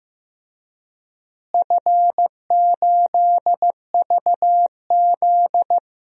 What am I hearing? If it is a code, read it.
Morse code, heard as a keyed tone, about 16 wpm, 700 Hz. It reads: F8VZ
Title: DXCC-100 Call Signs - 4 Characters - Repeated 2x - 15wpm